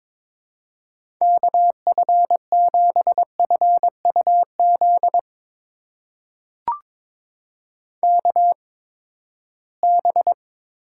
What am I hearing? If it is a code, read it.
Morse code, heard as a keyed tone, about 22 wpm, spaced as for 15 wpm, 700 Hz. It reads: KF7FUZ KB
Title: USA Call Signs - 6 Characters - 8x Character Spacing - 22wpm